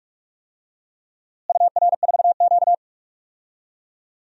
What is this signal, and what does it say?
Morse code, heard as a keyed tone, about 46 wpm, 700 Hz. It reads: UR4Q
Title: DXCC-100 Call Signs - 4 Characters - Repeated 2x - 45wpm